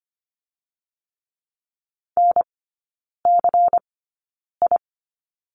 Morse code, heard as a keyed tone, about 25 wpm, spaced as for 9 wpm, 700 Hz. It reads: NCI
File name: Lesson 17e - Letter u - Three Characters 25wpm